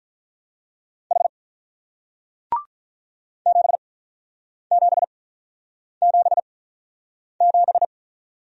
Morse code, heard as a keyed tone, about 47 wpm, spaced as for 9 wpm, 690 Hz. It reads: H 7777